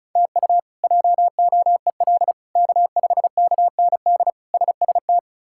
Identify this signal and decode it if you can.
Morse code, heard as a keyed tone, about 36 wpm, 700 Hz. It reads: TU JOEL K5KND SST
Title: K1USN SST QSO Practice - 35wpm